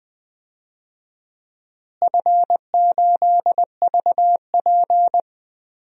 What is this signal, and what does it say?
Morse code, heard as a keyed tone, about 20 wpm, 700 Hz. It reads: F8VP
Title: DXCC-100 Call Signs - 4 Characters - Repeated 3x - 20wpm